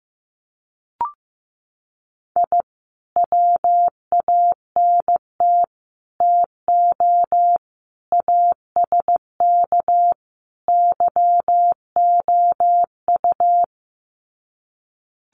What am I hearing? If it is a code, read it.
Morse code, heard as a keyed tone, about 15 wpm, 700 Hz. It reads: I WANT TO ASK YOU
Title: Sets of 5 Words 15wpm